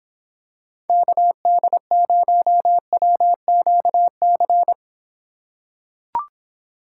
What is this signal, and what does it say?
Morse code, heard as a keyed tone, about 26 wpm, 700 Hz. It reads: KD0WQC